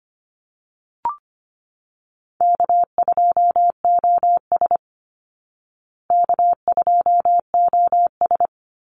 Morse code, heard as a keyed tone, about 25 wpm, 700 Hz. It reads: K2OS K2OS
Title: Call Signs Repeated 2x - 25wpm